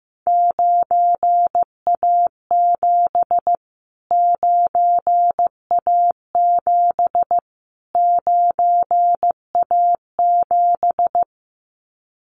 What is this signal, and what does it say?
Morse code, heard as a keyed tone, about 15 wpm, 700 Hz. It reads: 9A7 9A7 9A7
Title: Callsign Prefixes for Top 25 DXCC - Repeated 3x - 15wpm